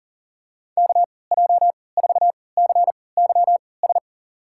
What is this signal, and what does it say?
Morse code, heard as a keyed tone, about 40 wpm, spaced as for 23 wpm, 700 Hz. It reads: KJ4CYS
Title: USA Call Signs - 6 Characters - 3x Character Spacing - 40wpm